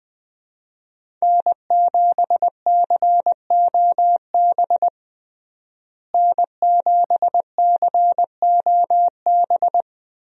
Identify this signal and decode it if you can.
Morse code, heard as a keyed tone, about 20 wpm, 700 Hz. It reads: N7COB N7COB
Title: USA Call Signs - 5 Characters - Repeated 3x - 20wpm